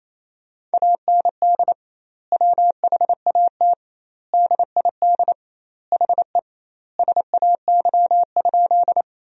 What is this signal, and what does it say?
Morse code, heard as a keyed tone, about 28 wpm, 700 Hz. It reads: AND WHAT DID HE SAY?
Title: 2x Word-Spacing - Sentences from Top 200 Words - 28wpm